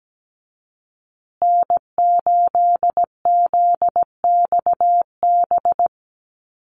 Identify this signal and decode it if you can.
Morse code, heard as a keyed tone, about 17 wpm, 700 Hz. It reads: N8ZXB
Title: USA Call Signs Repeated 3x 17wpm